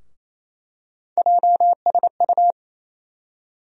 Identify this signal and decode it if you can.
Morse code, heard as a keyed tone, about 27 wpm, 700 Hz. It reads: JSU
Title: USA Call Sign Postfixes - 3 Letters 28wpm